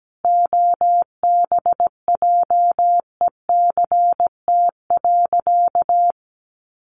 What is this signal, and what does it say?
Morse code, heard as a keyed tone, about 17 wpm, 700 Hz. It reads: OBJECT.
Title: Sentences from Top 1400 Words 17wpm